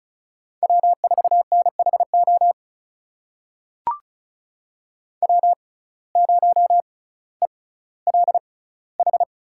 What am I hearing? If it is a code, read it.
Morse code, heard as a keyed tone, about 36 wpm, spaced as for 18 wpm, 700 Hz. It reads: W4NHO W0ELH
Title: USA Call Signs - 5 Characters - 6x Character Spacing - 35wpm